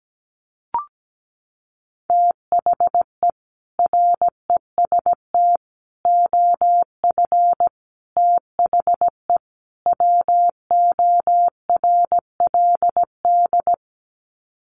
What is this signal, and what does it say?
Morse code, heard as a keyed tone, about 17 wpm, 700 Hz. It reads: THE REST OF THE WORLD
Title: Sets of 5 Words 17wpm